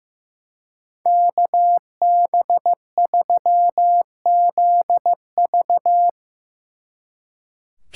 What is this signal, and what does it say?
Morse code, heard as a keyed tone, about 16 wpm, 700 Hz. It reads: KB3ZV
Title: Speed Racing - USA 2x2 Call Signs 15wpm